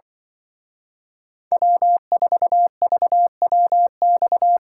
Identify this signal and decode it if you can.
Morse code, heard as a keyed tone, about 23 wpm, 700 Hz. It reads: W4VWX
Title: Speed Racing - USA 1x3 Call Signs 20wpm